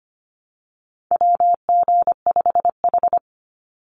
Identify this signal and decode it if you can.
Morse code, heard as a keyed tone, about 25 wpm, 700 Hz. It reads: WG5H
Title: Call Signs Repeated 3x - 25wpm